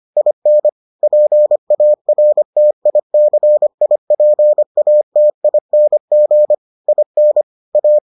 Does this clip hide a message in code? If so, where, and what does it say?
Yes: Morse code from 0.2 to 8.0 s, heard as a keyed tone, about 25 wpm, 600 Hz. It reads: IN PARTICIPATING IN A